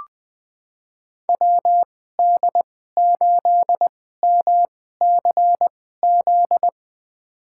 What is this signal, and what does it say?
Morse code, heard as a keyed tone, about 20 wpm, spaced as for 14 wpm, 700 Hz. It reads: WD8MCZ
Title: USA Call Signs - 6 Characters - 2x Character Spacing - 20wpm